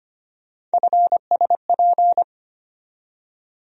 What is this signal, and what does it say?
Morse code, heard as a keyed tone, about 25 wpm, 700 Hz. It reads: FSP
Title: USA Call Sign Postfixes - 3 Letters 25wpm